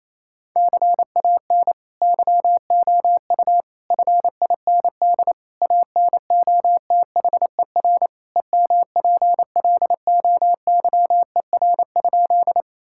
Morse code, heard as a keyed tone, about 28 wpm, 700 Hz. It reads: CAN YOU FIND ANOTHER EMPLOYER?
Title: Sentences from Top 1600 Words 28wpm